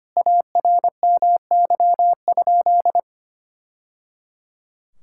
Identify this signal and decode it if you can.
Morse code, heard as a keyed tone, about 25 wpm, 700 Hz. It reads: ARMY?
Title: Sentences from Top 1000 Words - Review Part 5 - 25wpm